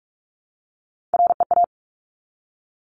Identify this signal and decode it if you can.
Morse code, heard as a keyed tone, about 46 wpm, 700 Hz. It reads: REA
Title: Top Three Letter Combinations in Words 45wpm